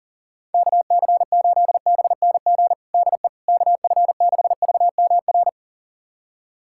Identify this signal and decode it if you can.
Morse code, heard as a keyed tone, about 40 wpm, 690 Hz. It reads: KC8BNG DE KF6VMR
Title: USA Callsign DE Callsign - 6 Characters - 40wpm